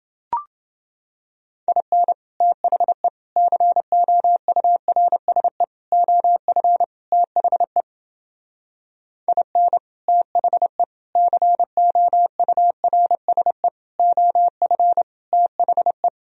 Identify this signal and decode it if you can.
Morse code, heard as a keyed tone, about 30 wpm, 700 Hz. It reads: IN THE COURSE OF THE IN THE COURSE OF THE